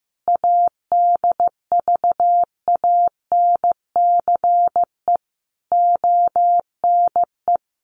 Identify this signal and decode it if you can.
Morse code, heard as a keyed tone, about 15 wpm, 700 Hz. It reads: ADVANCE ONE
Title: Sentences from Top 1400 Words 15wpm